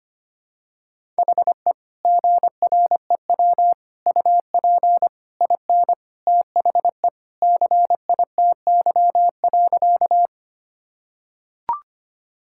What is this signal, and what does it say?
Morse code, heard as a keyed tone, about 25 wpm, 700 Hz. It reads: HE GREW UP IN THE CITY.
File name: practice / 25wpm / Sentences with 3-Word Phrases - 25wpm